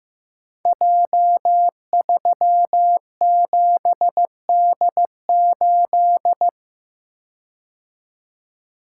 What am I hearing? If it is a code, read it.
Morse code, heard as a keyed tone, about 15 wpm, 700 Hz. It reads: J37D8
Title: Code Groups - 5 Letter:Numbers - 15wpm